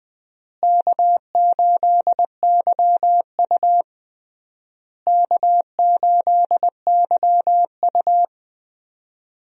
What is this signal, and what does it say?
Morse code, heard as a keyed tone, about 21 wpm, 700 Hz. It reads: K8YU K8YU
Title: USA Call Signs - 4 Characters - Repeated 3x - 20wpm